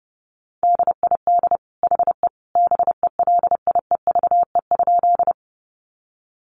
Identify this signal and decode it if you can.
Morse code, heard as a keyed tone, about 30 wpm, 700 Hz. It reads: DID HE BELIEVE?
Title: Sentences from Top 1000 Words 30wpm